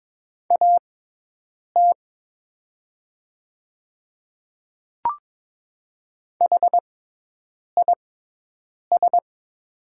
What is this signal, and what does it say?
Morse code, heard as a keyed tone, about 22 wpm, spaced as for 8 wpm, 700 Hz. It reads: AT HIS
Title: 6x Character Spacing - Top 200 Words - 22wpm